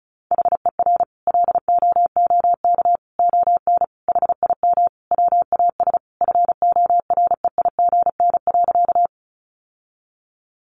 Morse code, heard as a keyed tone, about 34 wpm, 700 Hz. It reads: HER LOOK ON HIM WAS FOREIGN.
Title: Sentences from Top 1400 Words 35wpm